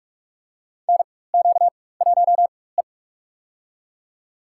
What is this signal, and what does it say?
Morse code, heard as a keyed tone, about 44 wpm, spaced as for 21 wpm, 690 Hz. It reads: NQ1E
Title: USA Call Signs - 4 Characters - 4x Character Spacing - 45wpm